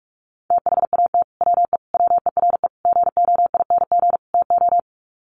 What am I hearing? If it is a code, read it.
Morse code, heard as a keyed tone, about 44 wpm, 700 Hz. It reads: THAT WE WERE GOING TO
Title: Sets of 5 Words Encore2 45wpm